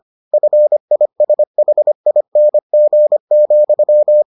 Morse code, heard as a keyed tone, about 25 wpm, 600 Hz. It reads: FISHING,